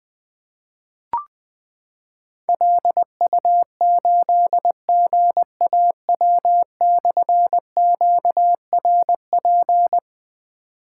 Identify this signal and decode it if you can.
Morse code, heard as a keyed tone, about 20 wpm, 700 Hz. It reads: LU8GAW/QRP
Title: Difficult Call Signs 20wpm